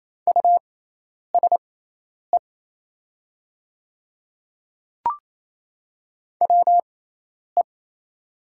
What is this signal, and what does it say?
Morse code, heard as a keyed tone, about 28 wpm, spaced as for 10 wpm, 690 Hz. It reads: USE WE
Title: 6x Character Spacing - Top 100 Words - 28wpm